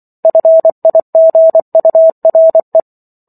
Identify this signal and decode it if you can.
Morse code, heard as a keyed tone, about 24 wpm, 650 Hz. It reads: FIGURE
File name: Book One / 15-Chapter 15 - 24 WPM